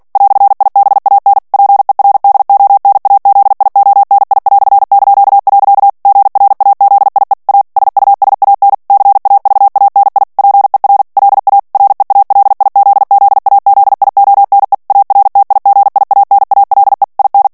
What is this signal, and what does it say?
Morse code, heard as a keyed tone, about 48 wpm, 800 Hz. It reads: "IBAN PER DONAZIONI"). GRAZIE A SUSAN CAVANI PER LA REALIZZAZIONE ARTIGIANALE IN
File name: morse mp3 / ITALIEN texte clair 1 vitesse 45 mots minute